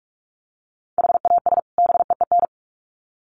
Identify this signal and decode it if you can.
Morse code, heard as a keyed tone, about 44 wpm, 700 Hz. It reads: HAS BEEN
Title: Sets of 2 Words 45wpm